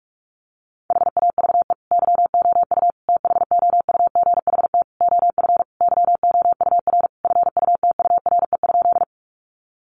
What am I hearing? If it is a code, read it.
Morse code, heard as a keyed tone, about 44 wpm, 700 Hz. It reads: HAVE YOU THOUGHT OF YOUR FUTURE?
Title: Sentences from Top 700 Words 45wpm